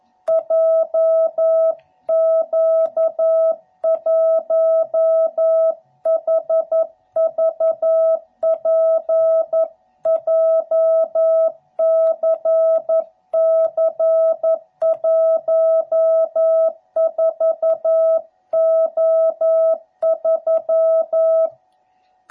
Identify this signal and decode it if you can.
Morse code, heard as a keyed tone, about 11 wpm, 660 Hz. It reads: JQ1HVP JCC14O3
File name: jh8cbh / 2021cw / cw12b